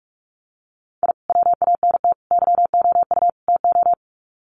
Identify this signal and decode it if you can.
Morse code, heard as a keyed tone, about 45 wpm, 700 Hz. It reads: I WANT YOU TO
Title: Sets of 4 Words 45wpm